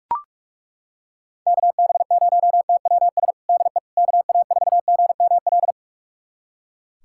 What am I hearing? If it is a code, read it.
Morse code, heard as a keyed tone, about 42 wpm, 700 Hz. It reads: KB0TWS DE KA4GML
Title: USA Callsign DE Callsign - 6 Characters - 45wpm